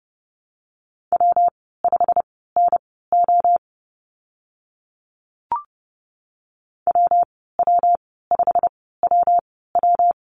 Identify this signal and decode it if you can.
Morse code, heard as a keyed tone, about 30 wpm, spaced as for 17 wpm, 700 Hz. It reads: W5NO WW5WW
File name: Lesson 19g - Letter w - Call Signs 30wpm